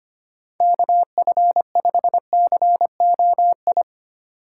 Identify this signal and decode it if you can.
Morse code, heard as a keyed tone, about 25 wpm, 700 Hz. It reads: KF5COI
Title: USA Call Signs - 6 Characters - 4x Character Spacing - 25wpm